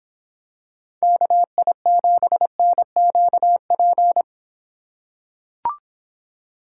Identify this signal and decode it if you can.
Morse code, heard as a keyed tone, about 24 wpm, 700 Hz. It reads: KI7NQP